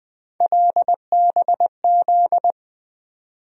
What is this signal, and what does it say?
Morse code, heard as a keyed tone, about 20 wpm, 700 Hz. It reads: LBZ
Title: USA Call Sign Postfixes - 3 Letters 20wpm